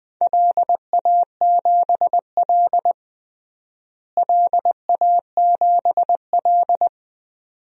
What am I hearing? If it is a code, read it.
Morse code, heard as a keyed tone, about 20 wpm, 700 Hz. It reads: LA7L LA7L
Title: DXCC-100 Call Signs - 4 Characters - Repeated 2x - 20wpm